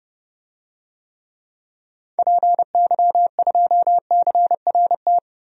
Morse code, heard as a keyed tone, about 31 wpm, 700 Hz. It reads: PY2CRT
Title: DXCC-100 Call Signs - 6 Characters - 30wpm